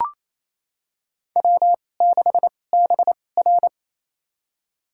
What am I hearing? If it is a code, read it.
Morse code, heard as a keyed tone, about 28 wpm, spaced as for 20 wpm, 700 Hz. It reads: W6BR
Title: USA Call Signs - 4 Characters - 2x Character Spacing - 28wpm